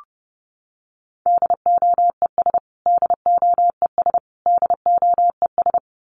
Morse code, heard as a keyed tone, about 30 wpm, 700 Hz. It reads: DOES DOES DOES
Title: Top 200 Words Repeated 30wpm